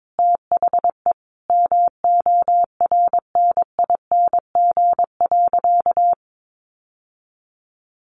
Morse code, heard as a keyed tone, about 22 wpm, 700 Hz. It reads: THE MORNING.
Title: Sentences from Top 1300 Words 22wpm